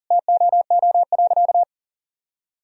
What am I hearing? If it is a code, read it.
Morse code, heard as a keyed tone, about 40 wpm, 700 Hz. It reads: TOO.
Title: Sentences from Top 1000 Words - Review Part 3 - 40wpm